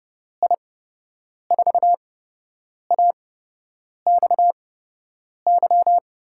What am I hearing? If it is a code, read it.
Morse code, heard as a keyed tone, about 30 wpm, spaced as for 8 wpm, 700 Hz. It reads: I4AXY
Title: USA Call Signs - 8x Character Spacing - 30wpm